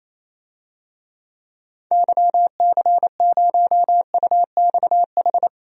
Morse code, heard as a keyed tone, about 28 wpm, 700 Hz. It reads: YC0UXH
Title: DXCC-100 Call Signs - 6 Characters - 28wpm